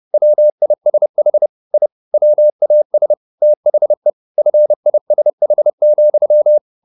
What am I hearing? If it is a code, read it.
Morse code, heard as a keyed tone, about 30 wpm, 600 Hz. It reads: WISH I WAS THE FISH,
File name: the old man and the sea / the old man and the sea 1 30wpm